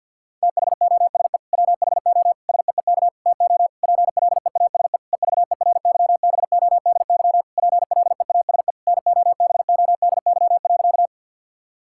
Morse code, heard as a keyed tone, about 50 wpm, 700 Hz. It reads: THOSE WHO SEEK TO PLEASE EVERYBODY PLEASE NOBODY.
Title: Sentences from Top 1600 Words 50wpm